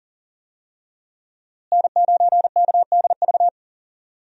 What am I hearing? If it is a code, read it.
Morse code, heard as a keyed tone, about 40 wpm, 700 Hz. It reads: N9KDV
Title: USA Call Signs - 5 Characters - Repeated 2x - 40wpm